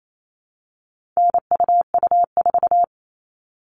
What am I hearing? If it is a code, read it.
Morse code, heard as a keyed tone, about 28 wpm, 700 Hz. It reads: NUU4
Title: All Cut Numbers - 4-Digit 28wpm